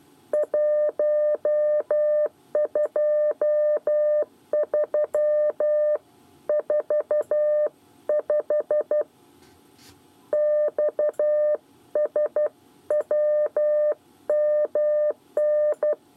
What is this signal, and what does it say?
Morse code, heard as a keyed tone, about 12 wpm, 600 Hz. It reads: 123 45 XSWMN
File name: 8 bis lezione cw 1